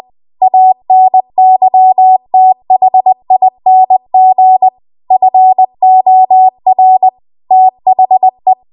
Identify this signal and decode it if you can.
Morse code, heard as a keyed tone, about 20 wpm, 750 Hz. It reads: ANYTHING FOR THE